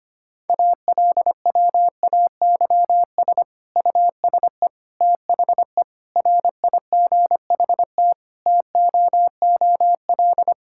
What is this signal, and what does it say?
Morse code, heard as a keyed tone, about 25 wpm, 700 Hz. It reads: ALWAYS USE THE RIGHT TOOL
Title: Sentences from Top 1000 Words - Review Part 6 - 25wpm